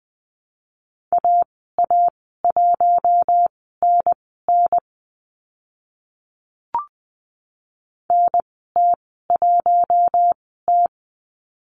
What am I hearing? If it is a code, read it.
Morse code, heard as a keyed tone, about 20 wpm, spaced as for 15 wpm, 700 Hz. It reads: AA1NN NT1T
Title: 042 Lesson 8g - Number 1 - Call Signs 20wpm